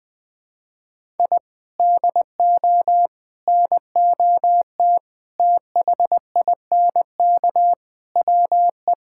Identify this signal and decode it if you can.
Morse code, heard as a keyed tone, about 20 wpm, 700 Hz. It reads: I DO NOT THINK WE
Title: Sentences with 4-Word Phrases - 20wpm